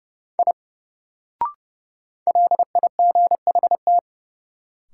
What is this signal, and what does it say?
Morse code, heard as a keyed tone, about 30 wpm, 700 Hz. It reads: I LIGHT
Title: Sentences from Top 200 Words - New Word Drill - 30wpm